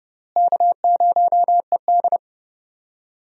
Morse code, heard as a keyed tone, about 32 wpm, 700 Hz. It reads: K0ED
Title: USA Call Signs - 4 Characters - Repeated 3x - 30wpm